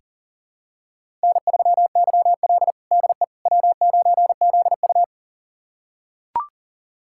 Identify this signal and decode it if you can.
Morse code, heard as a keyed tone, about 40 wpm, 700 Hz. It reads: N3YL DE W9ZU
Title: USA Callsign DE Callsign - 4 Characters - 40wpm